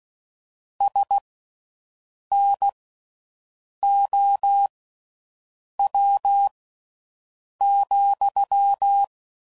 Morse code, heard as a keyed tone, about 16 wpm, spaced as for 6 wpm, 800 Hz. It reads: SNOW,